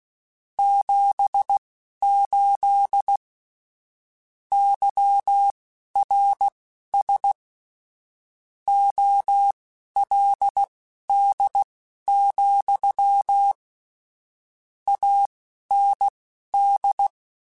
Morse code, heard as a keyed tone, about 16 wpm, spaced as for 11 wpm, 800 Hz. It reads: 78 YRS OLD, AND